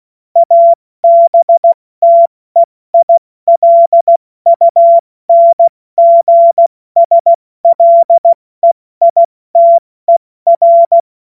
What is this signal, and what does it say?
Morse code, heard as a keyed tone, about 16 wpm, 680 Hz. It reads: ABTEILUNGSLEITER
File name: Morsekurs / Witze / Witz 06. Dezember 2022 70 F80